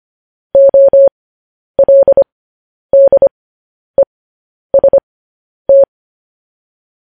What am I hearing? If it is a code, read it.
Morse code, heard as a keyed tone, about 25 wpm, spaced as for 10 wpm, 550 Hz. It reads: OLDEST
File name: podcast / News headlines-1x-25x10